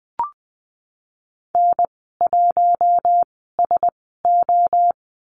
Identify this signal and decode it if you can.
Morse code, heard as a keyed tone, about 20 wpm, spaced as for 15 wpm, 700 Hz. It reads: N1SO
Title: 042 Lesson 8g - Number 1 - Call Signs 20wpm